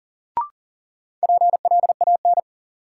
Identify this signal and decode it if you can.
Morse code, heard as a keyed tone, about 37 wpm, 700 Hz. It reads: PLAN
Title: Sentences from Top 500 Words - New Word Drill - 40wpm